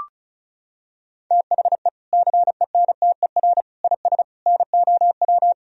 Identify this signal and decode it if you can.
Morse code, heard as a keyed tone, about 34 wpm, 700 Hz. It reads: THE CENTER IS NOW
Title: Sentences from Top 1000 Words - Review Part 2 - 35wpm